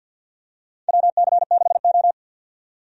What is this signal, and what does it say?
Morse code, heard as a keyed tone, about 51 wpm, 700 Hz. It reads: WC6O
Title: USA Call Signs - 4 Characters - 50wpm